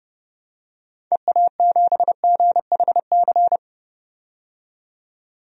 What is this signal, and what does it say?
Morse code, heard as a keyed tone, about 32 wpm, 700 Hz. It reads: EA7GHC
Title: DXCC-100 Call Signs - 6 Characters - 30wpm